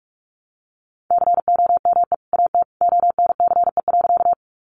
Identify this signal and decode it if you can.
Morse code, heard as a keyed tone, about 44 wpm, 700 Hz. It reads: COME AT ONCE.
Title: Sentences from Top 300 Words 45wpm